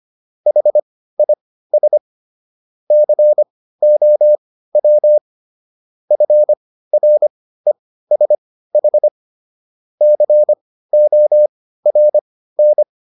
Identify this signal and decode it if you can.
Morse code, heard as a keyed tone, about 25 wpm, spaced as for 15 wpm, 600 Hz. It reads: HIS COW FRESH CORN